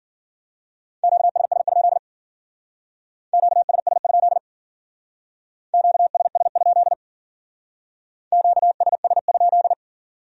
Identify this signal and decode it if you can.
Morse code, heard as a keyed tone, about 53 wpm, 700 Hz. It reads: QSS? QSS? QSS? QSS?